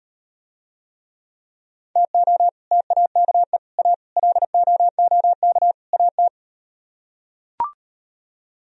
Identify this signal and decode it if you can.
Morse code, heard as a keyed tone, about 38 wpm, 700 Hz. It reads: TO TAKE A LOOK AT